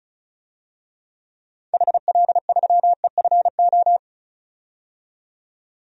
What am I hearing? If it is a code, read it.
Morse code, heard as a keyed tone, about 34 wpm, 700 Hz. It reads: HL3EFO
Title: DXCC-100 Call Signs - 6 Characters - 35wpm